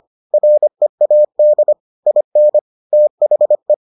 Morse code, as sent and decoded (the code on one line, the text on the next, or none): .-. .- -.. / .. -. / - .... .
READ IN THE